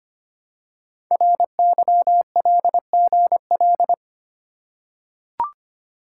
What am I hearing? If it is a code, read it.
Morse code, heard as a keyed tone, about 25 wpm, 700 Hz. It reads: RYLGL
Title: Code Groups - 5 Letters - 25wpm